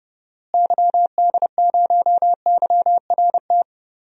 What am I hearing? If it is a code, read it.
Morse code, heard as a keyed tone, about 30 wpm, 700 Hz. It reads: YD0YRT
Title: DXCC-100 Call Signs - 6 Characters - Repeated 2x - 30wpm